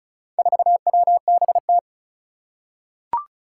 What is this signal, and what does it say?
Morse code, heard as a keyed tone, about 36 wpm, 700 Hz. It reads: 4WBT
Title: Four Challenging Letter-Number Combinations 35wpm